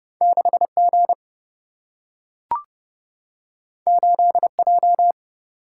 Morse code, heard as a keyed tone, about 30 wpm, 700 Hz. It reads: 6G 8J
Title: Single-Letter Single-Number 30wpm